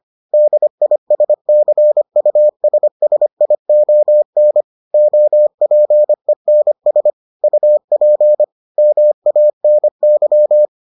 Morse code, heard as a keyed tone, about 25 wpm, 600 Hz. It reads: DISCUSSION OPENS UP MANY